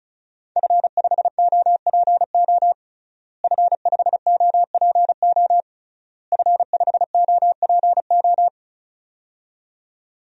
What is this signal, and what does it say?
Morse code, heard as a keyed tone, about 36 wpm, 700 Hz. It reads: F5OPO F5OPO F5OPO
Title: DXCC-100 Call Signs - 5 Characters - Repeated 3x - 35wpm